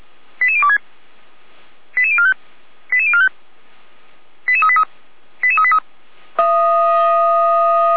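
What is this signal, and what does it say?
Five-tone selective call (ZVEI), heard as quick runs of five tones, tones of about 70 ms. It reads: 88026 88045 88045 88383 88383
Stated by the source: ZVEI five-tone selective call 88102 received on an SDR, Feuerwehr Orlamünde.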